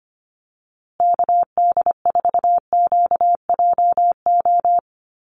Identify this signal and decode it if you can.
Morse code, heard as a keyed tone, about 25 wpm, 700 Hz. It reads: KD4QJO
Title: USA Call Signs Repeated 2x 25wpm